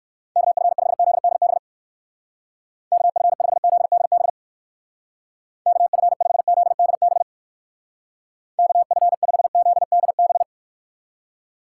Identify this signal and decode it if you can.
Morse code, heard as a keyed tone, about 59 wpm, 700 Hz. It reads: KF5ZDB KF5ZDB KF5ZDB KF5ZDB